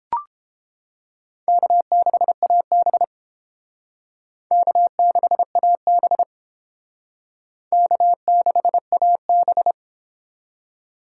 Morse code, heard as a keyed tone, about 29 wpm, 700 Hz. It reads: K6AB K6AB K6AB